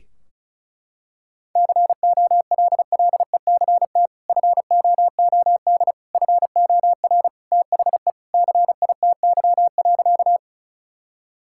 Morse code, heard as a keyed tone, about 34 wpm, 700 Hz. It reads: COLLECT FOOD FOR THE CITY.
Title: Sentences from Top 1600 Words 35wpm